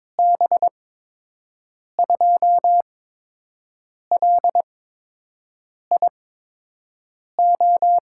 Morse code, heard as a keyed tone, about 22 wpm, spaced as for 6 wpm, 700 Hz. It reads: B2LIO
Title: USA Call Signs - 6 Characters - 8x Character Spacing - 22wpm